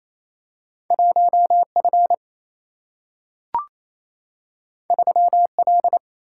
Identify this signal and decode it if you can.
Morse code, heard as a keyed tone, about 28 wpm, 700 Hz. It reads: 1F 3L
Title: Single-Letter Single-Number 28wpm